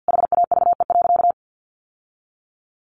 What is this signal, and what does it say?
Morse code, heard as a keyed tone, about 50 wpm, 700 Hz. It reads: HAVE.
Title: Sentences from Top 1200 Words 50wpm